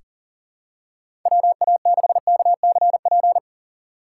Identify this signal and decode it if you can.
Morse code, heard as a keyed tone, about 40 wpm, 700 Hz. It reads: WA6KCP
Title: USA Call Signs - 6 Characters - 4x Character Spacing - 40wpm